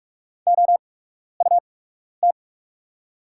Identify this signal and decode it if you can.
Morse code, heard as a keyed tone, about 48 wpm, spaced as for 12 wpm, 700 Hz. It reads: OUT
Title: 8x Character Spacing - Top 200 Words - 45wpm